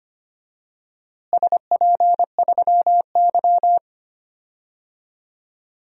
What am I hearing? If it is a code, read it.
Morse code, heard as a keyed tone, about 25 wpm, 700 Hz. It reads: SP3Y
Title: DXCC-100 Call Signs - 4 Characters - Repeated 3x - 25wpm